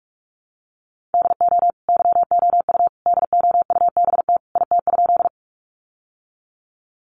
Morse code, heard as a keyed tone, about 44 wpm, 700 Hz. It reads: DO YOU DOUBT IT?
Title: Sentences from Top 1500 Words 45wpm